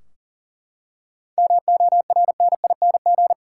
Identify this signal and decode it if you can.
Morse code, heard as a keyed tone, about 37 wpm, 700 Hz. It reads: MORNING
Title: Seven Letter Words - 40wpm